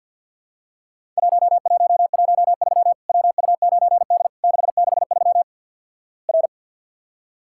Morse code, heard as a keyed tone, about 50 wpm, 700 Hz. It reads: 1113 WU9D 663 R
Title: Worked All Europe DX Contest - QTC Exchange - 50wpm